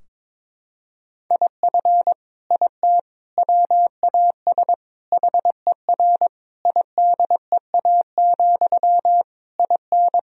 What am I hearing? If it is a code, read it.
Morse code, heard as a keyed tone, about 22 wpm, 700 Hz. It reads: IF IT WAS HER IDEA, IN
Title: Sentences from Top 1000 Words - Review Part 2 - 22wpm